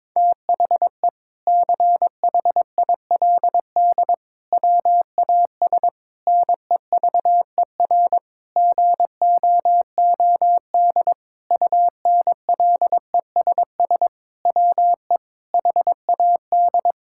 Morse code, heard as a keyed tone, about 22 wpm, 700 Hz. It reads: THE CHILD WAS NEVER GOOD UNLESS WE HAD
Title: Sentences from Top 1000 Words - Review Part 6 - 22wpm